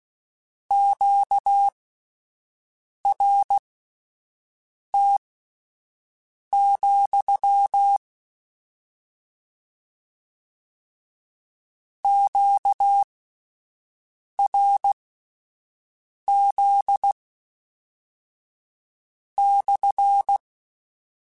Morse code, heard as a keyed tone, about 16 wpm, spaced as for 5 wpm, 800 Hz. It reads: QRT, QRZ /
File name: Q Calls / Q Calls-mp3s / Q CALLS SPSL @10